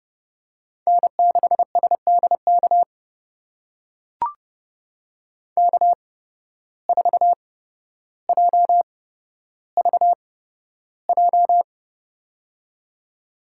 Morse code, heard as a keyed tone, about 32 wpm, spaced as for 13 wpm, 700 Hz. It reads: N6SDK K4JVJ
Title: USA Call Signs - 5 Characters - 8x Character Spacing - 30wpm